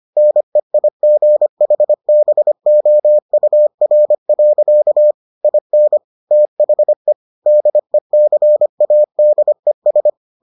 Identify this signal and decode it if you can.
Morse code, heard as a keyed tone, about 25 wpm, 600 Hz. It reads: NEIGHBOUR. IN THE DECADES